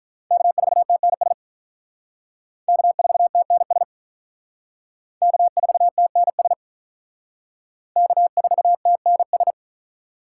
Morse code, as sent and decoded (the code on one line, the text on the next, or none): -.- ....- - -. ... / -.- ....- - -. ... / -.- ....- - -. ... / -.- ....- - -. ...
K4TNS K4TNS K4TNS K4TNS